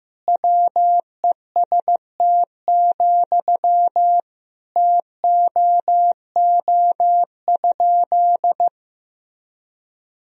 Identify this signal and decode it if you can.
Morse code, heard as a keyed tone, about 16 wpm, 700 Hz. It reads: WEST, TOO?
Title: Sentences from Top 1000 Words - Review Part 2 - 15wpm